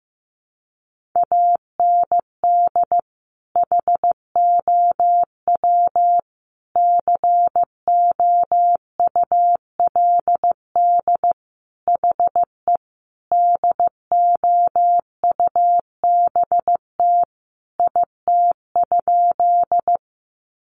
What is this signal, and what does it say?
Morse code, heard as a keyed tone, about 15 wpm, 700 Hz. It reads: AND HOW COULD HE DOUBT IT?
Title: Sentences from Top 1500 Words 15wpm